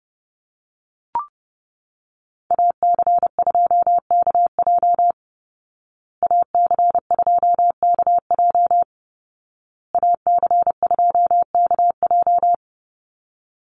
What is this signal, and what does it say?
Morse code, heard as a keyed tone, about 28 wpm, 700 Hz. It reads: AC2KJ AC2KJ AC2KJ
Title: Call Signs Repeated 2x - 30wpm